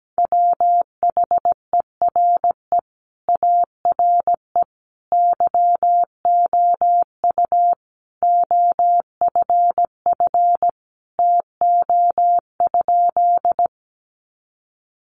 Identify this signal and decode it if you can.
Morse code, heard as a keyed tone, about 17 wpm, 700 Hz. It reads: WHERE ARE YOU OFF TO?
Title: Sentences from Top 200 Words 17wpm